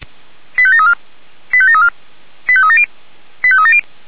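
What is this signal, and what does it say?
Five-tone selective call (ZVEI), heard as quick runs of five tones, tones of about 70 ms. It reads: 85734 85734 85370 85370
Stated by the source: ZVEI five-tone selective call, received by SDR.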